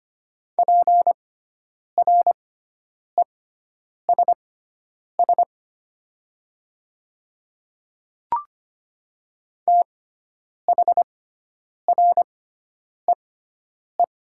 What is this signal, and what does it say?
Morse code, heard as a keyed tone, about 25 wpm, spaced as for 9 wpm, 700 Hz. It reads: PRESS THREE
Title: 6x Character Spacing - Top 300 Words - 25wpm